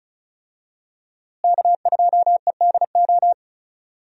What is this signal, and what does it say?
Morse code, heard as a keyed tone, about 36 wpm, 700 Hz. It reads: K2EDO
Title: USA Call Signs - 5 Characters - Repeated 3x - 35wpm